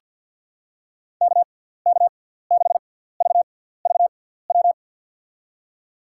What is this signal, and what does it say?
Morse code, heard as a keyed tone, about 48 wpm, spaced as for 17 wpm, 700 Hz. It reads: KK6VVW
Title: USA Call Signs - 6 Characters - 6x Character Spacing - 50wpm